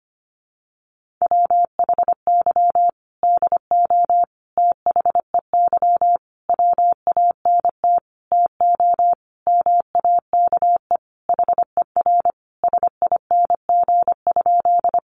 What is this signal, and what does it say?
Morse code, heard as a keyed tone, about 25 wpm, 700 Hz. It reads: WHY DO THEY WANT TO MAKE HER SING?
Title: Sentences from Top 400 Words 25wpm